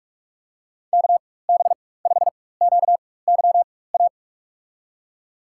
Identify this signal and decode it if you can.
Morse code, heard as a keyed tone, about 46 wpm, spaced as for 21 wpm, 700 Hz. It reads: KB5QYA
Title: USA Call Signs - 6 Characters - 4x Character Spacing - 45wpm